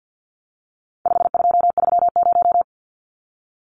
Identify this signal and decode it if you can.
Morse code, heard as a keyed tone, about 50 wpm, 700 Hz. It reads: 5230
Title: Numbers - 4-Digit 50wpm